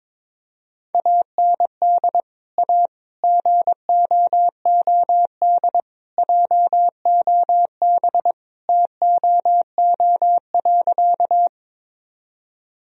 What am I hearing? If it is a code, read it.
Morse code, heard as a keyed tone, about 22 wpm, 700 Hz. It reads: AND A GOOD JOB TOO.
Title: Sentences from Top 1000 Words - Review Part 3 - 22wpm